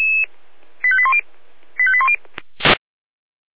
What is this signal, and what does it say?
Five-tone selective call (ZVEI), heard as quick runs of five tones, tones of about 70 ms. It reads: 85810 85810